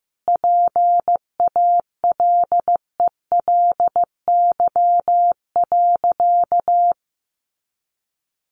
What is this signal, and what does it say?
Morse code, heard as a keyed tone, about 15 wpm, 700 Hz. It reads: PALELY.
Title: A Sound Of Thunder 15wpm